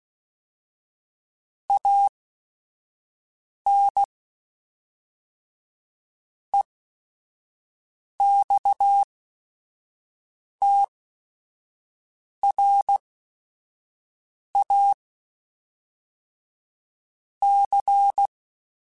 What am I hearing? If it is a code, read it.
Morse code, heard as a keyed tone, about 16 wpm, spaced as for 5 wpm, 800 Hz. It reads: AN EXTRA C